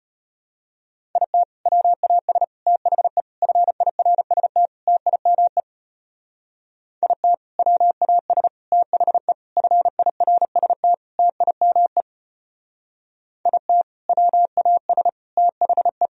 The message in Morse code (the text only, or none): IT WAS THE FIRST TIME IT WAS THE FIRST TIME IT WAS THE